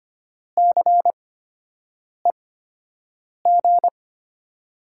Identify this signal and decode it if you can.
Morse code, heard as a keyed tone, about 25 wpm, spaced as for 7 wpm, 700 Hz. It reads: CEG
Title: USA Call Signs - 6 Characters - 8x Character Spacing - 25wpm